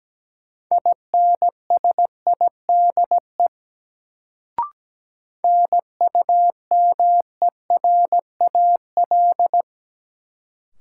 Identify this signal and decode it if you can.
Morse code, heard as a keyed tone, about 17 wpm, 700 Hz. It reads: INSIDE NUMERAL
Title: Sentences from Top 1000 Words - Review Part 2 - New Word Drill - 17wpm